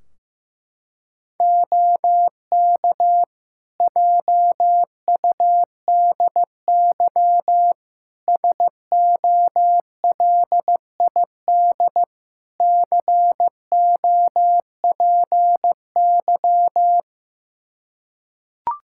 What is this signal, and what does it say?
Morse code, heard as a keyed tone, about 15 wpm, 700 Hz. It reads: OK JUDY SOLID COPY
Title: Practice QSO - Exchange 3 - More Info - 15wpm